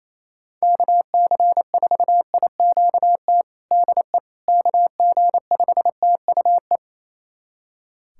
Morse code, heard as a keyed tone, about 28 wpm, 700 Hz. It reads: KC4IQT DE KG5TUE
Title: USA Callsign DE Callsign 28wpm